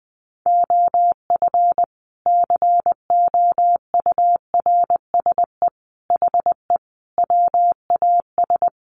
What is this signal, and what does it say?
Morse code, heard as a keyed tone, about 20 wpm, 700 Hz. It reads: OF COURSE HE WAS
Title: Sentences from Top 500 Words 20wpm